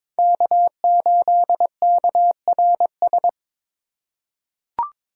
Morse code, heard as a keyed tone, about 22 wpm, 700 Hz. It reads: K8KRS
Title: USA Call Signs - 5 Characters - 4x Character Spacing - 22wpm